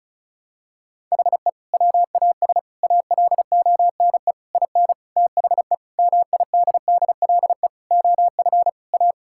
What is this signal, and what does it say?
Morse code, heard as a keyed tone, about 36 wpm, 700 Hz. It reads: HE WAS ALONE IN THE MIDDLE OF A
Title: Sentences from Top 1000 Words - Review Part 6 - 35wpm